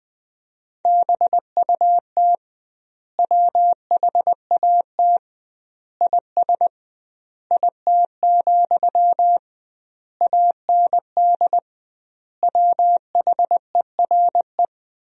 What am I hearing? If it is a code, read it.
Morse code, heard as a keyed tone, about 21 wpm, 700 Hz. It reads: BUT WHAT IS IT, AND WHERE
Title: 2x Word-Spacing - Sentences from Top 300 Words - 20wpm